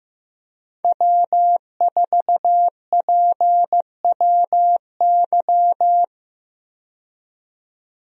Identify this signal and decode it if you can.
Morse code, heard as a keyed tone, about 15 wpm, 700 Hz. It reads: W4PWY
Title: USA Call Signs - 5 Characters - 15wpm